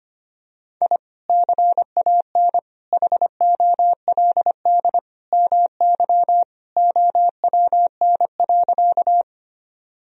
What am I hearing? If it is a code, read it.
Morse code, heard as a keyed tone, about 25 wpm, 700 Hz. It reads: I CAN HOLD MY OWN.
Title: Sentences from Top 1000 Words - Review Part 2 - 25wpm